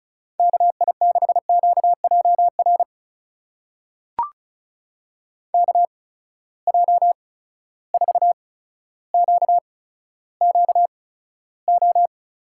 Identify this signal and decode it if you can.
Morse code, heard as a keyed tone, about 36 wpm, spaced as for 15 wpm, 700 Hz. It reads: KI6QJR KJ4QQO
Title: USA Call Signs - 6 Characters - 8x Character Spacing - 35wpm